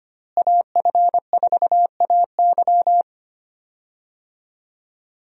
Morse code, heard as a keyed tone, about 25 wpm, 700 Hz. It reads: AF4AY
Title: USA Call Signs - 5 Characters - Repeated 3x - 25wpm